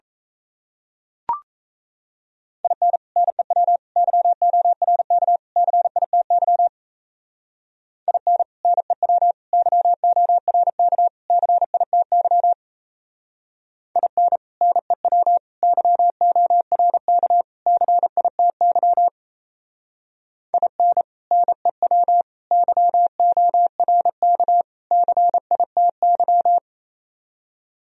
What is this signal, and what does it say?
Morse code, heard as a keyed tone, about 33 wpm, 700 Hz. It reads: IN NEW YORK CITY IN NEW YORK CITY IN NEW YORK CITY IN NEW YORK CITY
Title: Speed-Racing - Sets of 4 Words 28wpm